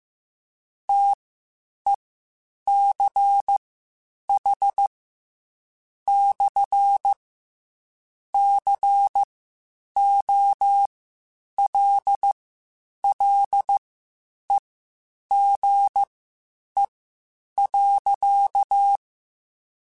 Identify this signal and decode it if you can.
Morse code, heard as a keyed tone, about 15 wpm, spaced as for 8 wpm, 800 Hz. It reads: TECH/COLLEGE.
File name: (spaced) / QSOs Spaced-mp3s / WD0BGP 1 @13 sp